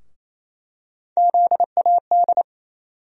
Morse code, heard as a keyed tone, about 28 wpm, 700 Hz. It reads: ZAD
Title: USA Call Sign Postfixes - 3 Letters 28wpm